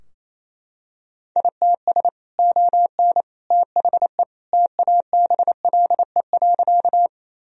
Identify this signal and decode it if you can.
Morse code, heard as a keyed tone, about 28 wpm, 700 Hz. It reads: ITS ON THE TABLE.
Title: Sentences from Top 1000 Words - Review Part 2 - 28wpm